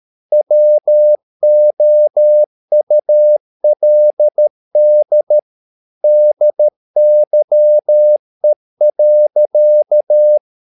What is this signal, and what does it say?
Morse code, heard as a keyed tone, about 13 wpm, 600 Hz. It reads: WOULD DYE.